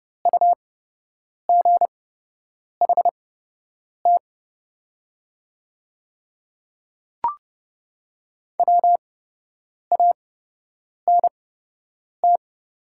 Morse code, heard as a keyed tone, about 30 wpm, spaced as for 8 wpm, 700 Hz. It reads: UGHT WANT
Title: 8x Character Spacing - Top 500 Words - 30wpm